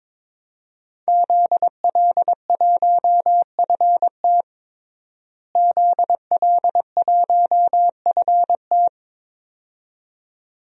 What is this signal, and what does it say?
Morse code, heard as a keyed tone, about 22 wpm, 700 Hz. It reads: ZL1FT ZL1FT
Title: DXCC-100 Call Signs - 5 Characters - Repeated 2x - 22wpm